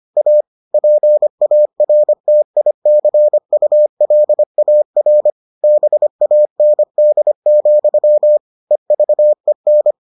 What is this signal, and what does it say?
Morse code, heard as a keyed tone, about 25 wpm, 600 Hz. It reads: A PARTICULAR BAND, EVEN